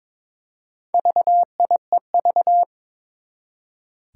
Morse code, heard as a keyed tone, about 22 wpm, 700 Hz. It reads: VIEV
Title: Four Challenging Letter-Number Combinations 22wpm